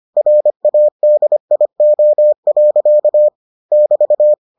Morse code, heard as a keyed tone, about 25 wpm, 600 Hz. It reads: RADIO. <BT>